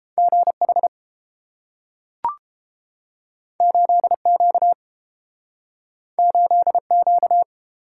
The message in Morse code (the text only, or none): GH 8Q 8Q